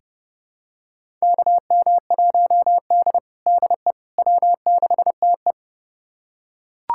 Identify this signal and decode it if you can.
Morse code, heard as a keyed tone, about 30 wpm, 700 Hz. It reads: KM1D DE W6TE
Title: USA Callsign DE Callsign - 4 Characters - 30wpm